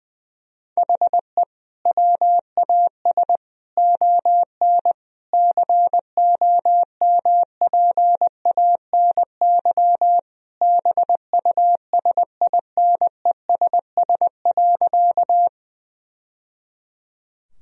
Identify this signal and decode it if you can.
Morse code, heard as a keyed tone, about 20 wpm, 700 Hz. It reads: HE WAS ON COMPANY BUSINESS.
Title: Sentences from Top 1000 Words - Review Part 3 - 20wpm